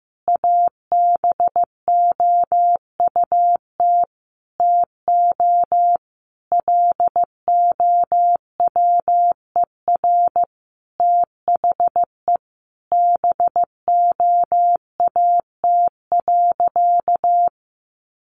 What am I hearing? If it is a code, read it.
Morse code, heard as a keyed tone, about 15 wpm, 700 Hz. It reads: ABOUT TO LOWER THE BOAT.
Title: Sentences from Top 700 Words 15wpm